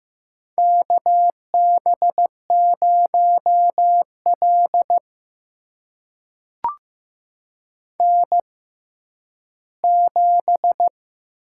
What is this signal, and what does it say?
Morse code, heard as a keyed tone, about 15 wpm, spaced as for 10 wpm, 700 Hz. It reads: KB0L N7
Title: USA Call Signs - 4 Characters - 6x Character Spacing - 15wpm